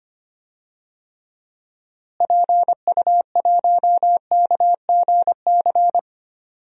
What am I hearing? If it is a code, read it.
Morse code, heard as a keyed tone, about 25 wpm, 700 Hz. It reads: PU1KGC
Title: DXCC-100 Call Signs - 6 Characters - Repeated 3x - 25wpm